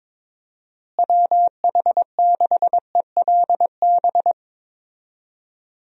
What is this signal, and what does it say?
Morse code, heard as a keyed tone, about 22 wpm, 700 Hz. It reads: WH6ELB
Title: USA Call Signs - 6 Characters - Repeated 3x - 22wpm